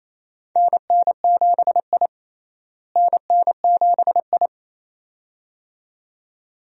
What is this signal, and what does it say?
Morse code, heard as a keyed tone, about 28 wpm, 700 Hz. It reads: NN7I NN7I
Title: USA Call Signs - 4 Characters - Repeated 2x - 28wpm